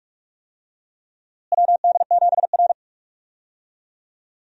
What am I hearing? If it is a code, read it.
Morse code, heard as a keyed tone, about 46 wpm, 700 Hz. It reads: WD7R
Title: USA Call Signs - 4 Characters - Repeated 3x - 45wpm